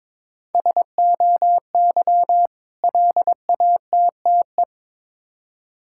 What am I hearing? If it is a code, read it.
Morse code, heard as a keyed tone, about 22 wpm, 700 Hz. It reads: SOY LATTE
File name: Difficult Words with Hints - 22wpm